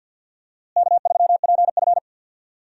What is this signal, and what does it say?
Morse code, heard as a keyed tone, about 46 wpm, 700 Hz. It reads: K3PF
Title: USA Call Signs - 4 Characters - Repeated 2x - 50wpm